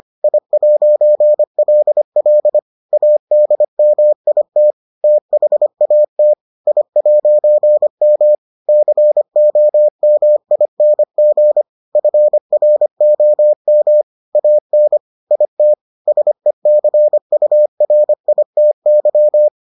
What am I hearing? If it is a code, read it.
Morse code, heard as a keyed tone, about 25 wpm, 600 Hz. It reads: I'LL ADMIT THAT I'M COMING FROM AN IT SECURITY